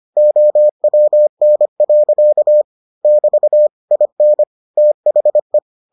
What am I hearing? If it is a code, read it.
Morse code, heard as a keyed tone, about 25 wpm, 600 Hz. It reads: OWN. <BT> IN THE